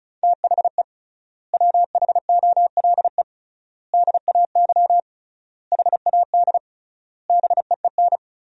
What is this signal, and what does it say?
Morse code, heard as a keyed tone, about 36 wpm, 700 Hz. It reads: THE WHOLE DAY HAD BEEN
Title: 3x Word-Spacing - Sentences from Top 500 Words - 35wpm